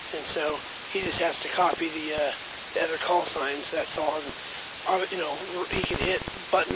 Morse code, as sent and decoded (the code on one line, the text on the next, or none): none